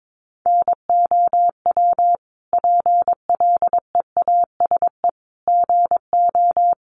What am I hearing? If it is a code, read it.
Morse code, heard as a keyed tone, about 22 wpm, 700 Hz. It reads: NOW PLEASE GO
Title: Sentences from Top 1500 Words 22wpm